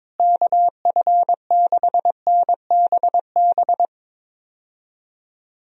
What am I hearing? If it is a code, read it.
Morse code, heard as a keyed tone, about 22 wpm, 700 Hz. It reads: KF6NBB
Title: USA Call Signs - 6 Characters - Repeated 3x - 22wpm